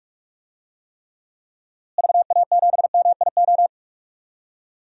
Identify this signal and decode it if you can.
Morse code, heard as a keyed tone, about 42 wpm, 700 Hz. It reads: VA7MIO
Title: DXCC-100 Call Signs - 6 Characters - Repeated 3x - 45wpm